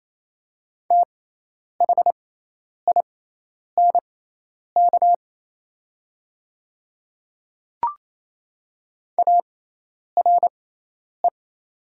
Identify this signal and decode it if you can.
Morse code, heard as a keyed tone, about 27 wpm, spaced as for 10 wpm, 700 Hz. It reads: THINK ARE
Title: 6x Character Spacing - Top 200 Words - 28wpm